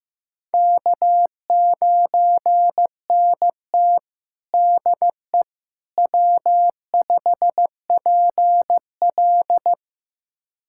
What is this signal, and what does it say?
Morse code, heard as a keyed tone, about 15 wpm, 700 Hz. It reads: K9NT DE W5PL
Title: USA Callsign DE Callsign - 4 Characters - 15wpm